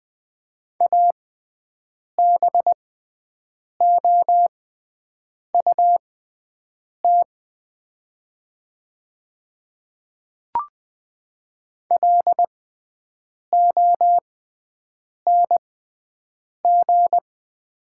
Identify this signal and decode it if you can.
Morse code, heard as a keyed tone, about 20 wpm, spaced as for 7 wpm, 700 Hz. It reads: ABOUT LONG
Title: 6x Character Spacing - Top 100 Words - 20wpm